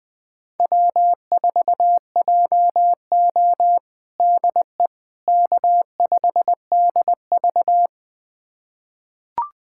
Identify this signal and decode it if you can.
Morse code, heard as a keyed tone, about 21 wpm, 700 Hz. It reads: W4JO DE K5DV
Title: USA Callsign DE Callsign - 4 Characters - 20wpm